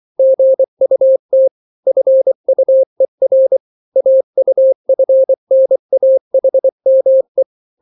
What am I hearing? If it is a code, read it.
Morse code, heard as a keyed tone, about 24 wpm, 530 Hz. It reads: GUT FUER AUFNAHME
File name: daily / 23wpm